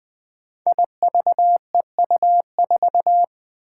Morse code, heard as a keyed tone, about 20 wpm, 700 Hz. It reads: IVEU4
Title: Five Challenging Letter-Number Combinations 20wpm